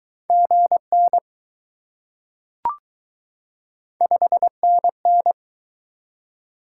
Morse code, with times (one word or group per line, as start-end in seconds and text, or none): 0.3-1.2 GN
4.0-5.3 5NN